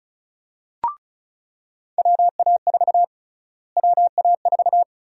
Morse code, heard as a keyed tone, about 36 wpm, 700 Hz. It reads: WA4 WA4
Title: USA Call Sign Prefixes - Repeated 2x - 35wpm